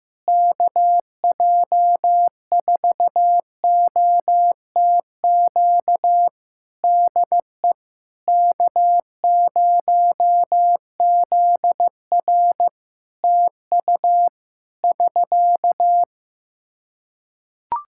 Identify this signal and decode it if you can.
Morse code, heard as a keyed tone, about 15 wpm, 700 Hz. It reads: KJ4OTQ DE K0ZR TU <SK>
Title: Practice QSO - Exchange 4 - Ending - 15wpm